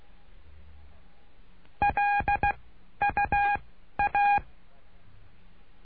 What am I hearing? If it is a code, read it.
Morse code, heard as a keyed tone, about 16 wpm, 820 Hz. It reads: L U A
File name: cursos / CURSO DE TELEGRAFIA / Telegrafia - Trilha 09